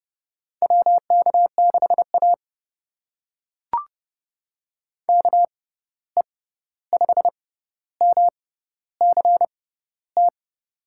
Morse code, heard as a keyed tone, about 30 wpm, spaced as for 14 wpm, 700 Hz. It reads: WK6A KE5MCT
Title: USA Call Signs - 6x Character Spacing - 30wpm